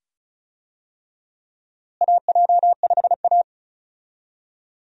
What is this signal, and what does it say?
Morse code, heard as a keyed tone, about 34 wpm, 700 Hz. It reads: AJ5A
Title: USA Call Signs - 4 Characters - Repeated 2x - 35wpm